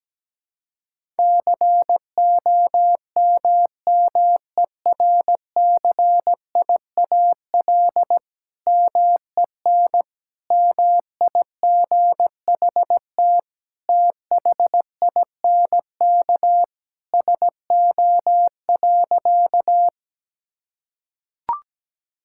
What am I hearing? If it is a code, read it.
Morse code, heard as a keyed tone, about 17 wpm, 700 Hz. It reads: COMMERCIAL MEN MIGHT THINK SO.
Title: Sentences from Top 1000 Words - Review Part 5 - 17wpm